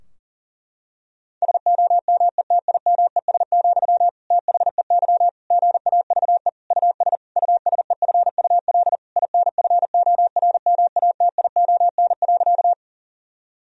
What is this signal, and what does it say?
Morse code, heard as a keyed tone, about 40 wpm, 700 Hz. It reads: SOMETIMES, THEY GAVE US USEFUL INFORMATION.
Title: Sentences from Top 1000 Words - Review Part 4 - 40wpm